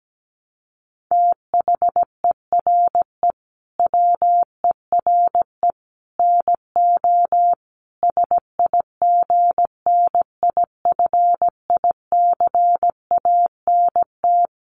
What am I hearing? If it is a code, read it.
Morse code, heard as a keyed tone, about 17 wpm, 700 Hz. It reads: THERE WERE NO SIGNIFICANT
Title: Sets of 5 Words Encore2 17wpm